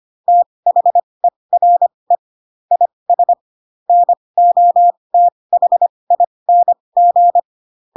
Morse code, heard as a keyed tone, about 25 wpm, spaced as for 20 wpm, 700 Hz. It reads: THERE IS NOTHING
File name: Pinocchio / Pinocchio ch7 15 25 ik1hge